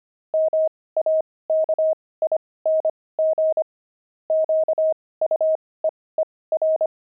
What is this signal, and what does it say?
Morse code, heard as a keyed tone, about 25 wpm, spaced as for 18 wpm, 630 Hz. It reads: MAKING QUEER
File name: pangram-fs-18-cs-25-explorer